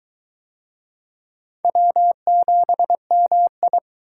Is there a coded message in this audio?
Yes: Morse code from 1.6 to 3.8 s, heard as a keyed tone, about 21 wpm, 700 Hz. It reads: W7MI